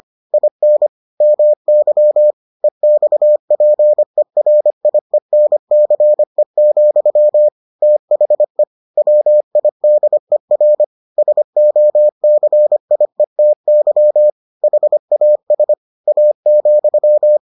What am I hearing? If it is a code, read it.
Morse code, heard as a keyed tone, about 25 wpm, 600 Hz. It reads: IN MY EXPERIENCE, THE WIDER SOCIETY HAS A,